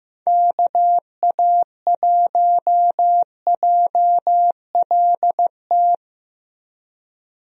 Morse code, heard as a keyed tone, about 15 wpm, 700 Hz. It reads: KA1JLT
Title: USA Call Signs - 6 Characters - Repeated 2x - 15wpm